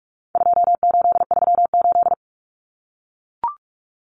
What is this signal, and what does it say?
Morse code, heard as a keyed tone, about 44 wpm, 700 Hz. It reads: 2838
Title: Common Cut Numbers - 4-Digit 45wpm